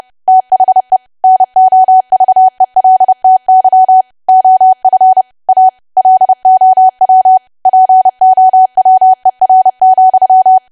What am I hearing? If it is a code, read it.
Morse code, heard as a keyed tone, about 30 wpm, 750 Hz. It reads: THE NOVELTY OF A LOW POWER,
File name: cwa / W1AW-Level3 / 30WPM